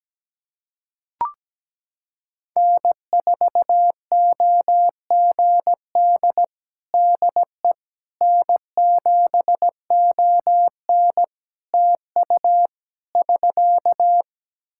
Morse code, heard as a keyed tone, about 17 wpm, 700 Hz. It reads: N4OGD DE N7ON TU <SK>
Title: Practice QSO - Exchange 4 - Ending - 17wpm